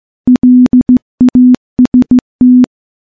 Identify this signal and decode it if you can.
Morse code, heard as a keyed tone, about 15 wpm, 250 Hz. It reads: LAST
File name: morse-training-15WPM